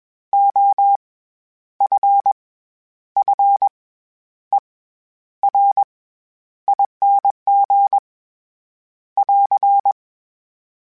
Morse code, heard as a keyed tone, about 21 wpm, 800 Hz. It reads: O F F E R ING <AR>
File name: Special INGs-mp3s / Special INGs-mp3s / SlowToFaster ing @20